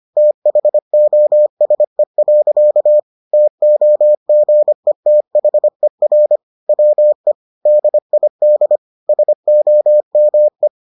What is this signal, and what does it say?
Morse code, heard as a keyed tone, about 25 wpm, 600 Hz. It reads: THOSE. TOGETHER WE DID SOME